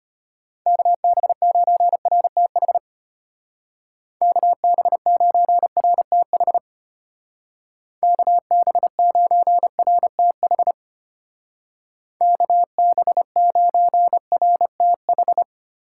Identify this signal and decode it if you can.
Morse code, heard as a keyed tone, about 32 wpm, 700 Hz. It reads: KB9RTH KB9RTH KB9RTH KB9RTH